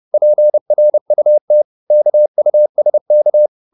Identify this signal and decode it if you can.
Morse code, heard as a keyed tone, about 30 wpm, 600 Hz. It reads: PRUT KUSK
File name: Korte ord DK 30wpm